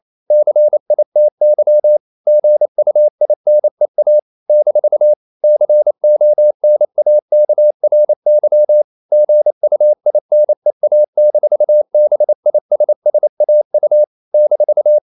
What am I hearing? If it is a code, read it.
Morse code, heard as a keyed tone, about 28 wpm, 600 Hz. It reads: CITY GUINEA - CONAKRY GUINEA-BISSAU -
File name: CW / Practice / NationalCapitals28